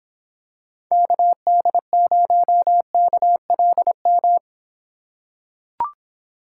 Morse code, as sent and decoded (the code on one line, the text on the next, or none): -.- -.. ----- -.- .-.. --
KD0KLM